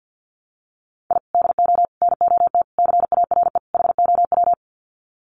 Morse code, heard as a keyed tone, about 50 wpm, 700 Hz. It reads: I DO NOT CARE HOW
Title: Sets of 5 Words Encore2 50wpm